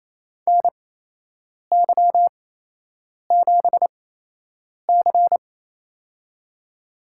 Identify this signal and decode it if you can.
Morse code, heard as a keyed tone, about 28 wpm, spaced as for 8 wpm, 700 Hz. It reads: NY7C
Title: USA Call Signs - 4 Characters - 8x Character Spacing - 28wpm